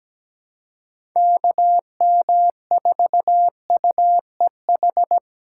Morse code, heard as a keyed tone, about 17 wpm, 700 Hz. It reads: KM4UEH
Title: USA Call Signs - 4x Character Spacing - 17wpm